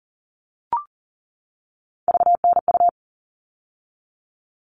Morse code, heard as a keyed tone, about 40 wpm, 700 Hz. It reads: VNU
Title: All Cut Numbers - 3-Digit 40wpm